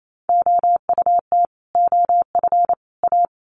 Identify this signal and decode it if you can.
Morse code, heard as a keyed tone, about 28 wpm, 700 Hz. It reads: OUT OF A
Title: Sets of 3 Words 28wpm